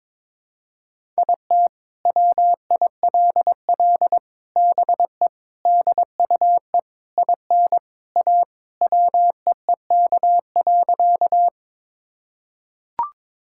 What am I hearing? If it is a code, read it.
Morse code, heard as a keyed tone, about 22 wpm, 700 Hz. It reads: IT WILL BE DUE IN A WEEK.
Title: Sentences from Top 1000 Words - Review Part 6 - 22wpm